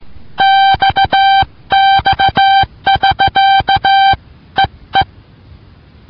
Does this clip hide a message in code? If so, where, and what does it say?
Yes: Morse code from 0.4 to 5.0 s, heard as a keyed tone, about 15 wpm, 780 Hz. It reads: XX<SK>EE